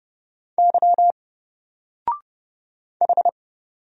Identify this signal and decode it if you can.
Morse code, heard as a keyed tone, about 30 wpm, 700 Hz. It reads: Y H
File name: Single-Letter-Number-30wpm